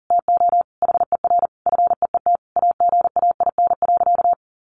Morse code, heard as a keyed tone, about 40 wpm, 700 Hz. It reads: TO HER FEET AGAIN.
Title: Sentences from Top 1300 Words 40wpm